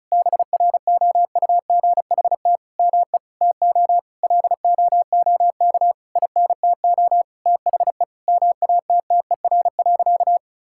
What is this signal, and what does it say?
Morse code, heard as a keyed tone, about 36 wpm, 700 Hz. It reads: BROUGHT ME TO LOOK INTO THE MATTER.
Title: Sentences from Top 1000 Words - Review Part 6 - 35wpm